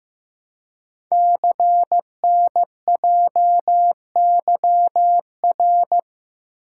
Morse code, heard as a keyed tone, about 15 wpm, 700 Hz. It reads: CNJYR
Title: Code Groups - 5 Letters - 15wpm